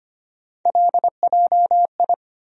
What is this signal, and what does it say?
Morse code, heard as a keyed tone, about 25 wpm, 700 Hz. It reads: LJI
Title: USA Call Sign Postfixes - 3 Letters 25wpm